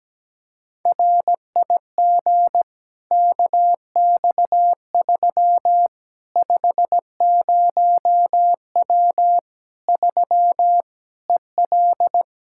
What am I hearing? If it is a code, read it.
Morse code, heard as a keyed tone, about 17 wpm, 700 Hz. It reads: RIG KX3 50W 3 EL
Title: Practice QSO - Exchange 2 - Info - 17wpm